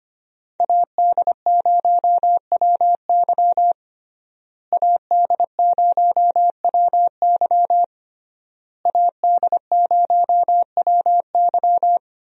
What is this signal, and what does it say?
Morse code, heard as a keyed tone, about 25 wpm, 700 Hz. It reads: AD0WY AD0WY AD0WY
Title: USA Call Signs - 5 Characters - Repeated 3x - 25wpm